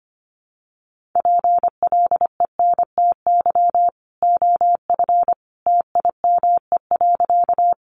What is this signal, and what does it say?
Morse code, heard as a keyed tone, about 25 wpm, 700 Hz. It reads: PLENTY OF TIME.
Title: Sentences from Top 1300 Words 25wpm